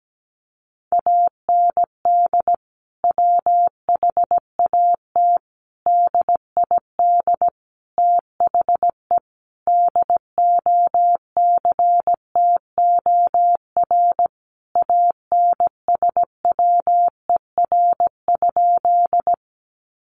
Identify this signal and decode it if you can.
Morse code, heard as a keyed tone, about 17 wpm, 700 Hz. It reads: AND WHAT DID THE DOCTOR ANSWER?
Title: Sentences from Top 1500 Words 17wpm